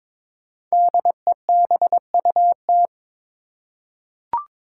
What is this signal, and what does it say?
Morse code, heard as a keyed tone, about 22 wpm, 700 Hz. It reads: DEBUT
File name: 5 Letter Words - First 3 Letters Given - 22wpm